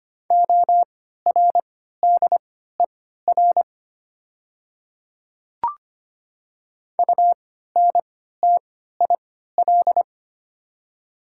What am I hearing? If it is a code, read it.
Morse code, heard as a keyed tone, about 25 wpm, spaced as for 14 wpm, 700 Hz. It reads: ORDER UNTIL
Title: 3x Character Spacing - Top 500 Words - 25wpm